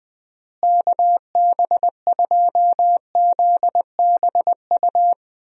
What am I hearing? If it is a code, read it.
Morse code, heard as a keyed tone, about 20 wpm, 700 Hz. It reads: KB2ZBU
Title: USA Call Signs - 6 Characters - Repeated 3x - 20wpm